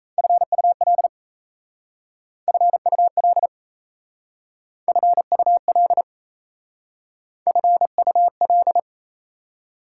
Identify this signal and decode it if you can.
Morse code, heard as a keyed tone, about 34 wpm, 700 Hz. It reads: FUL FUL FUL FUL
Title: Speed-Racing Prefixes and Suffixes 28wpm